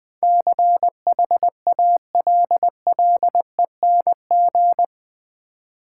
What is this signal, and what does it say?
Morse code, heard as a keyed tone, about 20 wpm, 700 Hz. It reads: CHALLENG
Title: Build A Word - 4 Letters At A Time - 11-Letter Words - 20wpm